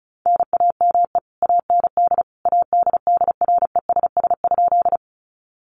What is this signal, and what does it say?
Morse code, heard as a keyed tone, about 34 wpm, 700 Hz. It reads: NAME AND ADDRESS?
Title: Sentences from Top 1000 Words 35wpm